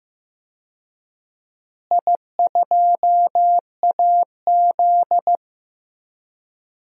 Morse code, heard as a keyed tone, about 15 wpm, 700 Hz. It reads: I2AZ
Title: DXCC-100 Call Signs - 4 Characters - Repeated 2x - 15wpm